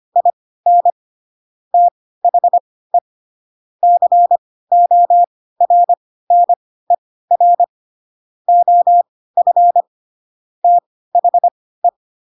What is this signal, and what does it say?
Morse code, heard as a keyed tone, about 25 wpm, spaced as for 16 wpm, 700 Hz. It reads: IN THE CORNER OF THE
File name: Pinocchio / Pinocchio ch7 10 25 ik1hge